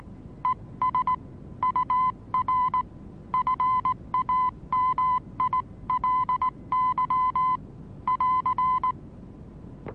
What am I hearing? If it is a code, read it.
Morse code, heard as a keyed tone, about 19 wpm, 1010 Hz. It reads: ES UR FAMILY <AR>
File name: repos / Morsekurs / DJ4UF Morsen26